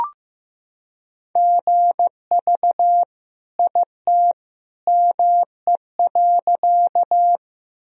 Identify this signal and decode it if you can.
Morse code, heard as a keyed tone, about 15 wpm, 700 Hz. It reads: GV IT ME.
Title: Sentences from Top 200 Words and Abbreviations 15wpm